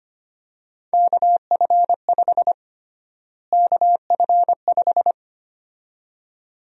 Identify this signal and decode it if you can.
Morse code, heard as a keyed tone, about 25 wpm, 700 Hz. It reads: KF5 KF5
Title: USA Call Sign Prefixes - Repeated 2x - 25wpm